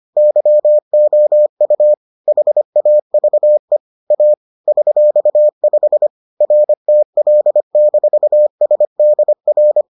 Morse code, heard as a keyed tone, about 25 wpm, 600 Hz. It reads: YOU HAVE A $5 RTL-SDR